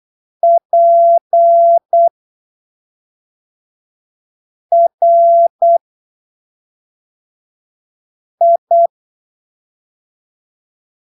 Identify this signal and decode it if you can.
Morse code, heard as a keyed tone, about 8 wpm, spaced as for 3 wpm, 680 Hz. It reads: PRI
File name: arquivos / texto / texto 29 08ppm